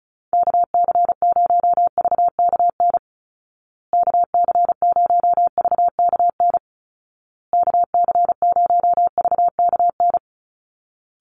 Morse code, heard as a keyed tone, about 34 wpm, 700 Hz. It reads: KC0VKN KC0VKN KC0VKN
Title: Call Signs Repeated 2x - 35wpm